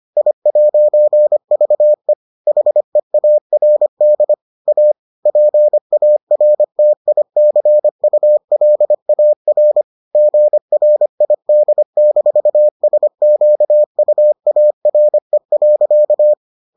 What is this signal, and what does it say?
Morse code, heard as a keyed tone, about 25 wpm, 600 Hz. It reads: I'VE HEARD A PARTICULAR GRID-SQUARE.